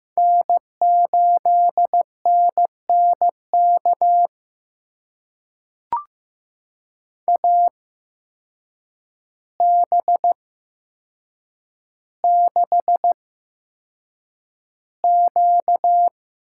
Morse code, heard as a keyed tone, about 15 wpm, spaced as for 7 wpm, 700 Hz. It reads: N8NNK AB6Q
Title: USA Call Signs - 8x Character Spacing - 15wpm